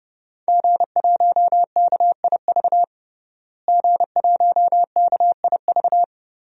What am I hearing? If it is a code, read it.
Morse code, heard as a keyed tone, about 30 wpm, 700 Hz. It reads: G1KIV G1KIV
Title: DXCC-100 Call Signs - 5 Characters - Repeated 2x - 30wpm